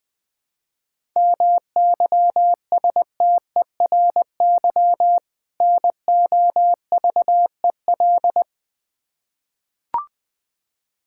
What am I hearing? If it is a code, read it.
Morse code, heard as a keyed tone, about 20 wpm, 700 Hz. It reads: MYSTERY NOVEL
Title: Difficult Words with Hints - 20wpm